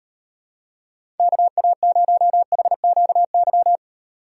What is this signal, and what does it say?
Morse code, heard as a keyed tone, about 38 wpm, 700 Hz. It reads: KA0HQY